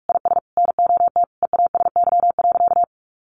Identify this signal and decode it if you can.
Morse code, heard as a keyed tone, about 44 wpm, 700 Hz. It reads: IS NOT EASY.
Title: Sentences from Top 1500 Words 45wpm